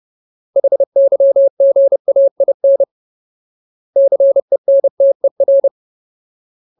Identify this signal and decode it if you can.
Morse code, heard as a keyed tone, about 30 wpm, 550 Hz. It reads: HYGAIN CENTER
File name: Ultimate Word List / 30WPM Ultimate